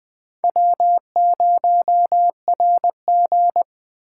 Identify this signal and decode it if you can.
Morse code, heard as a keyed tone, about 21 wpm, 700 Hz. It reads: W0RG